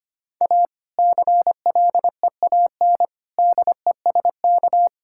Morse code, heard as a keyed tone, about 25 wpm, 700 Hz. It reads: A CLEAN DESK
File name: practice / 25wpm / Difficult Sentences with Hints - 25wpm